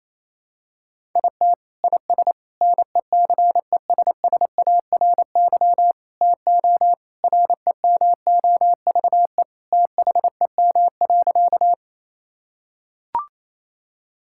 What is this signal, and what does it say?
Morse code, heard as a keyed tone, about 28 wpm, 700 Hz. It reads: IT IS NECESSARY TO REMOVE THEM.
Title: Sentences from Top 1000 Words - Review Part 5 - 28wpm